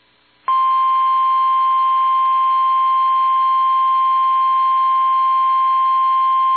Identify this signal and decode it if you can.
Emergency Alert System (SAME) tones (the 1050 Hz attention tone). It is heard from 0.8 to 5.9 s.